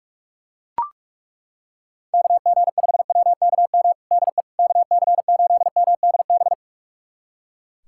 Morse code, heard as a keyed tone, about 41 wpm, 700 Hz. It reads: KG5WKM DE KC8MDB
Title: USA Callsign DE Callsign 45wpm